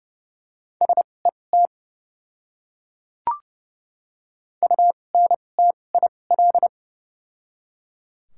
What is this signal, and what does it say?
Morse code, heard as a keyed tone, about 30 wpm, spaced as for 22 wpm, 700 Hz. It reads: SET UNTIL
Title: 2x Character Spacing - Top 300 Words - 30wpm